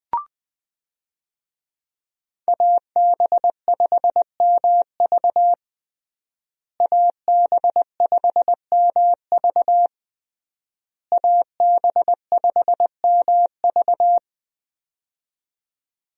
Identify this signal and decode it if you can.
Morse code, heard as a keyed tone, about 20 wpm, 700 Hz. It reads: AB5MV AB5MV AB5MV
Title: USA Call Signs - 5 Characters - Repeated 3x - 20wpm